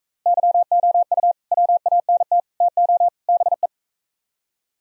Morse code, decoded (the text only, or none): YOU WANT TO BE